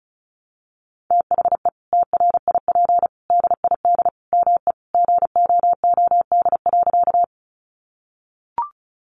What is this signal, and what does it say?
Morse code, heard as a keyed tone, about 34 wpm, 700 Hz. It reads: THE TRIP DID ME GOOD.
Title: Sentences from Top 1300 Words 35wpm